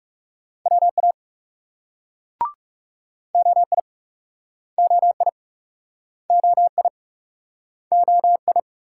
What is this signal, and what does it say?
Morse code, heard as a keyed tone, about 41 wpm, 700 Hz. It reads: WA OI OI OI OI